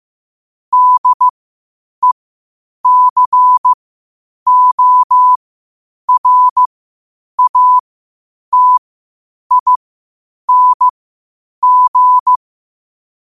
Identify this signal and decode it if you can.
Morse code, heard as a keyed tone, about 15 wpm, spaced as for 8 wpm, 1000 Hz. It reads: DECORATING
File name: qotdfiles / 8wpm